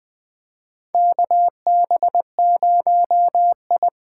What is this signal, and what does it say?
Morse code, heard as a keyed tone, about 20 wpm, 700 Hz. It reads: KB0I
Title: USA Call Signs - 6 Characters - 8x Character Spacing - 20wpm